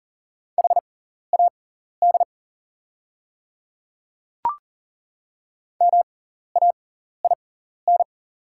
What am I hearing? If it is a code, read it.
Morse code, heard as a keyed tone, about 41 wpm, spaced as for 14 wpm, 690 Hz. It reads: HAD MAIN
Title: 6x Character Spacing - Top 300 Words - 40wpm